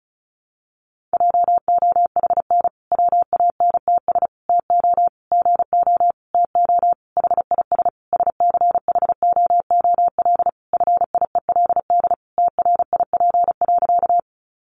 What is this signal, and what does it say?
Morse code, heard as a keyed tone, about 34 wpm, 700 Hz. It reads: JOHN WANTS TO GO TO HIS SCHOOL FIELD TRIP.
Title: Additional Easy Phrases 35wpm